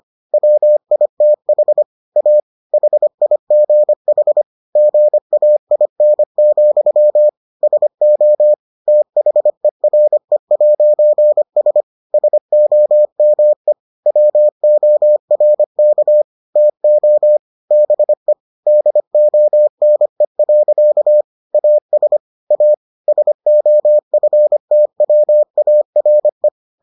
Morse code, heard as a keyed tone, about 25 wpm, 600 Hz. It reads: WITH A HIGH GAIN, SO THERE'S SOME WORK TO BE DONE. AS A SOFTWARE